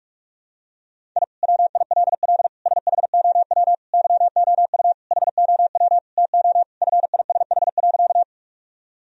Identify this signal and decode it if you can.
Morse code, heard as a keyed tone, about 48 wpm, 700 Hz. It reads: I WILL SHOW YOU HOW TO FISH.
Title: Sentences from Top 1000 Words - Review Part 1 - 45wpm